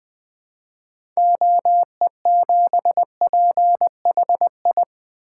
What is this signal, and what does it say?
Morse code, heard as a keyed tone, about 20 wpm, 700 Hz. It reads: OE7PHI
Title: 2023 SOTA Chaser Call Signs - 20wpm